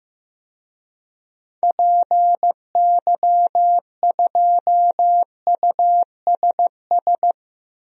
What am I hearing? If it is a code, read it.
Morse code, heard as a keyed tone, about 15 wpm, 700 Hz. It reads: PY2USS
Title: DXCC-100 Call Signs - 6 Characters - 15wpm